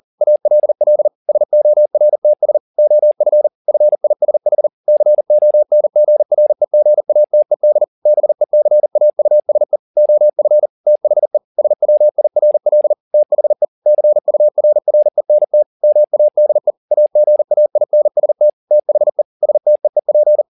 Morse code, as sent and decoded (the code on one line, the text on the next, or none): .- .-.. .-.. / ... --- .-. - ... / --- ..-. / ..-. .. ... .... / -.-. --- -. --. .-. --. .- - . -.. / -... -.-. .- ..- ... / --- ..-. / - .... / ... .-- .. .-. .-.. / - .... / -.-. ..- .-. .-. -. - / -- .- -.. / .- --. .- .. -. ... - / - .... / ... - . .--.
ALL SORTS OF FISH CONGREGATED BECAUSE OF THE SWIRL THE CURRENT MADE AGAINST THE STEEP